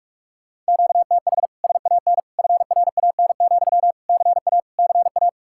Morse code, heard as a keyed tone, about 60 wpm, spaced as for 52 wpm, 700 Hz. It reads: QTH SAN FRAN, CA CA
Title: Speed-Racing Common QSO Phrases - 45wpm